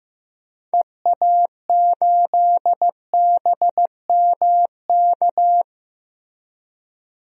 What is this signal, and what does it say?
Morse code, heard as a keyed tone, about 15 wpm, 700 Hz. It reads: EA8BMK
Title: DXCC-100 Call Signs - 6 Characters - Repeated 2x - 15wpm